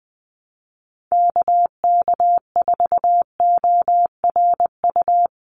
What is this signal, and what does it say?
Morse code, heard as a keyed tone, about 20 wpm, 700 Hz. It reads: KK4ORU
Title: USA Call Signs 20wpm